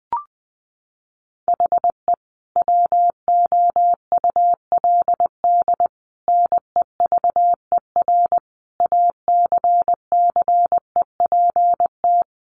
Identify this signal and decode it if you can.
Morse code, heard as a keyed tone, about 20 wpm, 700 Hz. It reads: HE WOULD NEVER ACCEPT
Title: Sentences from Top 1300 Words 20wpm